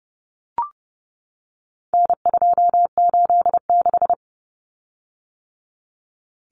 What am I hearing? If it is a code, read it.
Morse code, heard as a keyed tone, about 30 wpm, 700 Hz. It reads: N286
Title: Common Cut Numbers - 4-Digit 30wpm